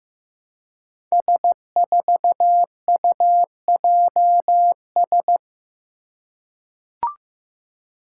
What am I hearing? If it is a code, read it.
Morse code, heard as a keyed tone, about 15 wpm, 700 Hz. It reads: S4UJS
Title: Five Challenging Letter-Number Combinations 15wpm